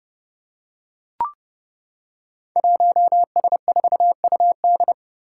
Morse code, heard as a keyed tone, about 30 wpm, 700 Hz. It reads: 1S4UD
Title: Five Challenging Letter-Number Combinations 30wpm